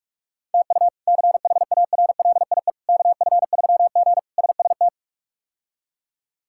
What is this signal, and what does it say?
Morse code, heard as a keyed tone, about 46 wpm, 700 Hz. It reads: TU CHARLIE KF3G SST
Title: K1USN SST QSO Practice - 45wpm